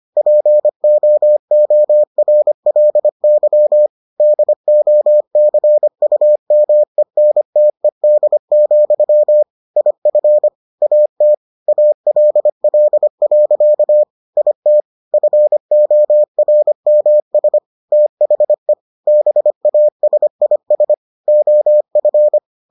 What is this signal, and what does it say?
Morse code, heard as a keyed tone, about 25 wpm, 600 Hz. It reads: POORLY DOCUMENTED, IF AT ALL. IT FORMS THE BASIS OF